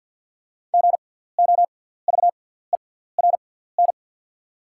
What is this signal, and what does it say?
Morse code, heard as a keyed tone, about 48 wpm, spaced as for 17 wpm, 690 Hz. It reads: GOVERN
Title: 6x Character Spacing - Top 500 Words - 50wpm